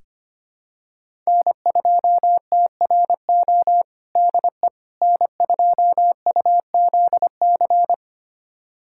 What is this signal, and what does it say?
Morse code, heard as a keyed tone, about 25 wpm, 700 Hz. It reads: N2TRO DE N2UZC
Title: USA Callsign DE Callsign - 5 Characters - 25wpm